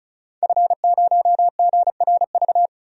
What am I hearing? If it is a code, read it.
Morse code, heard as a keyed tone, about 36 wpm, 700 Hz. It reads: F0GRV
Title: DXCC-100 Call Signs - 5 Characters - Repeated 2x - 35wpm